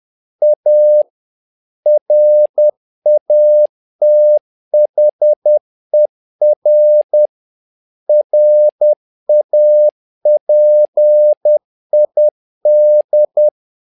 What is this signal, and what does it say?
Morse code, heard as a keyed tone, about 10 wpm, 600 Hz. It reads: A RATHER RAPID
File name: learning center / tower trouble / tower trouble 10wpm